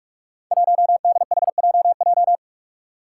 Morse code, heard as a keyed tone, about 44 wpm, 700 Hz. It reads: 1DHJJ
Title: Five Challenging Letter-Number Combinations 45wpm